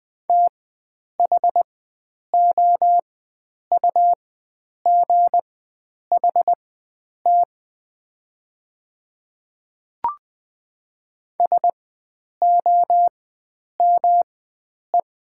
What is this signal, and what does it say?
Morse code, heard as a keyed tone, about 20 wpm, spaced as for 9 wpm, 700 Hz. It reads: THOUGHT SOME
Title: 4x Character Spacing - Top 300 Words - 20wpm